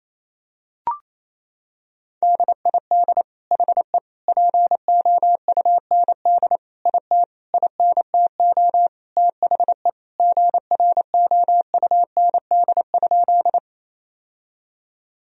Morse code, heard as a keyed tone, about 28 wpm, 690 Hz. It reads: DID HE POUND IT INTO THE GROUND?
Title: Sentences from Top 1000 Words - Review Part 2 - 28wpm